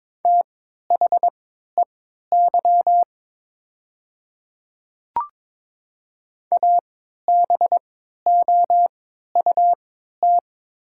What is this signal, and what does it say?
Morse code, heard as a keyed tone, about 22 wpm, spaced as for 12 wpm, 690 Hz. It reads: THEY ABOUT
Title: 3x Character Spacing - Top 100 Words - 22wpm